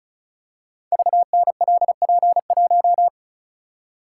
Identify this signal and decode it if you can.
Morse code, heard as a keyed tone, about 36 wpm, 700 Hz. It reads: VNLP1
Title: Code Groups - 5 Letter:Numbers - 35wpm